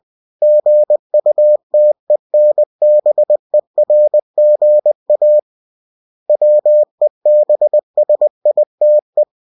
Morse code, heard as a keyed tone, about 20 wpm, 600 Hz. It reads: GUTENBERGA WEBSITE
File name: cw-books / winnie-the-pooh / winnie-20wpm-page-0116